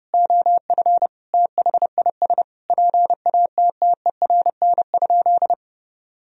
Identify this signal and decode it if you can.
Morse code, heard as a keyed tone, about 30 wpm, 700 Hz. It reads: OF THIS PATTERN?
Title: Sentences from Top 1000 Words - Review Part 2 - 30wpm